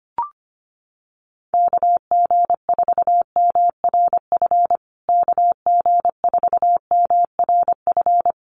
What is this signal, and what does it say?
Morse code, heard as a keyed tone, about 25 wpm, 700 Hz. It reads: KG4MRF KG4MRF
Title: USA Call Signs Repeated 2x 25wpm